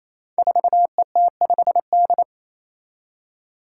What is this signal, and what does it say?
Morse code, heard as a keyed tone, about 27 wpm, 700 Hz. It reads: 4ET5D
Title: Five Challenging Letter-Number Combinations 28wpm